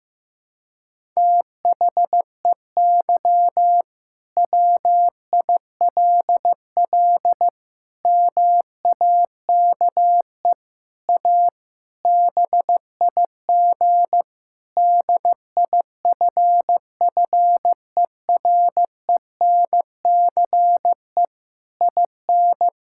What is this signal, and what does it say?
Morse code, heard as a keyed tone, about 15 wpm, 700 Hz. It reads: THEY WILL MAKE A BIG DIFFERENCE IN
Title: Sentences from Top 1000 Words - Review Part 5 - 15wpm